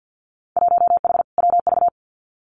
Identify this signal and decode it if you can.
Morse code, heard as a keyed tone, about 50 wpm, 700 Hz. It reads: 1H WV
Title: Winter Field Day Exchanges 50wpm